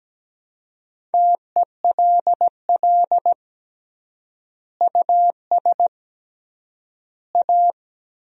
Morse code, heard as a keyed tone, about 17 wpm, 700 Hz. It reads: TELL US A
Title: 3x Word-Spacing - Sentences from Top 300 Words - 17wpm